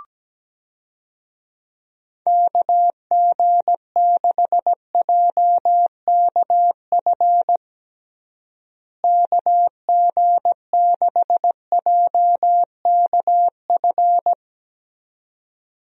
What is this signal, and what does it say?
Morse code, heard as a keyed tone, about 17 wpm, 700 Hz. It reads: KG6JKF KG6JKF
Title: USA Call Signs - 6 Characters - Repeated 2x - 17wpm